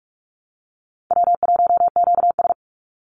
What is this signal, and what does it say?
Morse code, heard as a keyed tone, about 44 wpm, 700 Hz. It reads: W1QS
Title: Easy Call Signs Repeated 3x - 45wpm